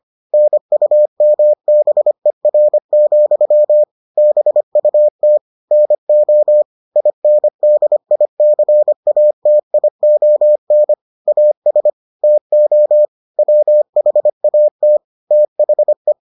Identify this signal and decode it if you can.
Morse code, heard as a keyed tone, about 25 wpm, 600 Hz. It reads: NUMBER, BUT NO INDICATION AS TO WHAT THE